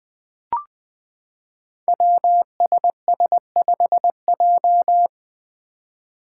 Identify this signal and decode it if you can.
Morse code, heard as a keyed tone, about 20 wpm, 700 Hz. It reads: WSS5J
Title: Five Challenging Letter-Number Combinations 20wpm